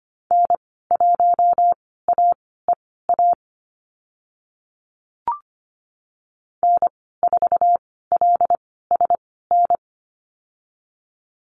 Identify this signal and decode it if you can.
Morse code, heard as a keyed tone, about 25 wpm, spaced as for 16 wpm, 700 Hz. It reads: N1AEA N4LSN
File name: Lesson 15g - Number 5 - Call Signs 25wpm